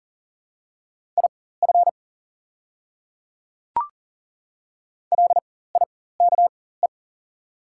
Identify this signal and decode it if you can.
Morse code, heard as a keyed tone, about 40 wpm, spaced as for 19 wpm, 700 Hz. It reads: IF LIKE
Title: 4x Character Spacing - Top 100 Words - 40wpm